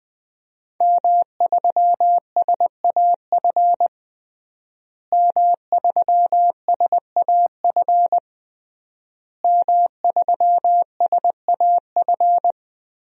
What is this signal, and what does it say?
Morse code, heard as a keyed tone, about 20 wpm, 700 Hz. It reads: M3SAF M3SAF M3SAF
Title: DXCC-100 Call Signs - 5 Characters - Repeated 3x - 20wpm